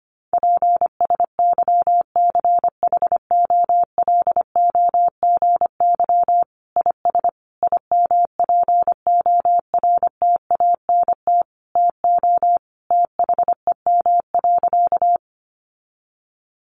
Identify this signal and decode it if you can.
Morse code, heard as a keyed tone, about 25 wpm, 700 Hz. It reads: PSYCHOLOGY IS IMPORTANT TO THEM.
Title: Sentences from Top 1400 Words 25wpm